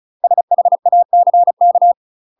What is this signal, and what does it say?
Morse code, heard as a keyed tone, about 36 wpm, 700 Hz. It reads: SHACK
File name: UK MASTER QSOs TRAINING / 35WPM / UK MASTER QSOs 0029